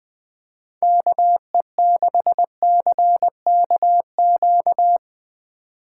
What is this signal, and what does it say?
Morse code, heard as a keyed tone, about 21 wpm, 700 Hz. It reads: KE6CKQ
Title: USA Call Signs - 6 Characters - 3x Character Spacing - 20wpm